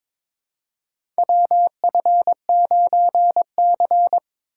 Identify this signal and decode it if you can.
Morse code, heard as a keyed tone, about 22 wpm, 700 Hz. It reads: WF9C
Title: USA Call Signs - 4 Characters - 3x Character Spacing - 22wpm